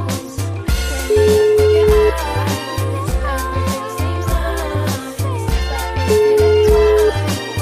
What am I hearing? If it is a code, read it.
Ringback tone (Europe, 425 Hz), with noise.